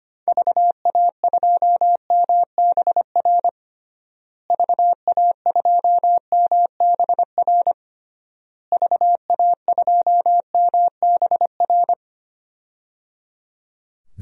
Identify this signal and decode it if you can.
Morse code, heard as a keyed tone, about 25 wpm, 700 Hz. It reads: VA2MBR VA2MBR VA2MBR
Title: DXCC-100 Call Signs - 6 Characters - Repeated 3x - 25wpm